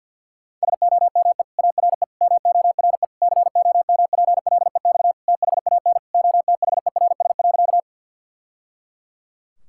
Morse code, heard as a keyed tone, about 47 wpm, 700 Hz. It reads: SOME ARE MORE COMPLEX THAN OTHERS.
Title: Sentences from Top 1000 Words - Review Part 6 - 50wpm